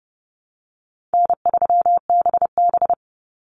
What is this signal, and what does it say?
Morse code, heard as a keyed tone, about 30 wpm, 700 Hz. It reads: N3BB
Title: Easy Call Signs - 30wpm